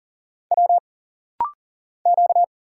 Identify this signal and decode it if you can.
Morse code, heard as a keyed tone, about 40 wpm, 700 Hz. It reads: W Q
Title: Single Letter-Number Alt - Rapid-Fire 40wpm